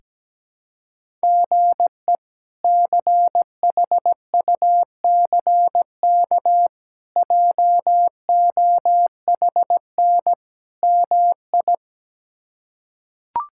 Morse code, heard as a keyed tone, about 17 wpm, 700 Hz. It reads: GE CHUCK JOHN MI
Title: K1USN SST QSO Practice - 17wpm